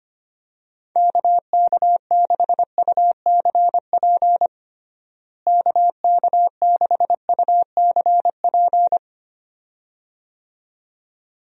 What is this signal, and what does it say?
Morse code, heard as a keyed tone, about 25 wpm, 700 Hz. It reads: KK6UCP KK6UCP
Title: USA Call Signs - 6 Characters - Repeated 2x - 25wpm